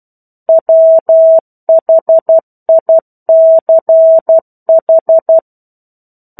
Morse code, heard as a keyed tone, about 12 wpm, 650 Hz. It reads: WHICH
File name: 01-Chapter-01 12 WPM